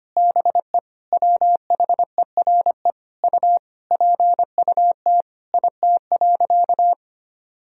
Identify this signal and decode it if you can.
Morse code, heard as a keyed tone, about 25 wpm, 700 Hz. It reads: BE WHERE U PUT IT.
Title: Sentences from Top 200 Words and Abbreviations 25wpm